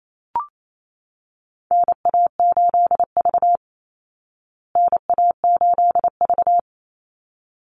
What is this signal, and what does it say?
Morse code, heard as a keyed tone, about 28 wpm, 700 Hz. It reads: NA8V NA8V
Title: Call Signs Repeated 3x - 28wpm